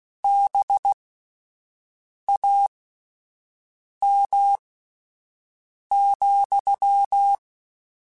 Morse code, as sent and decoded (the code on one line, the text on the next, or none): -... .- -- --..--
BAM,